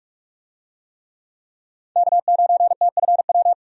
Morse code, heard as a keyed tone, about 46 wpm, 700 Hz. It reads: K9TFW
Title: USA Call Signs - 5 Characters - 45wpm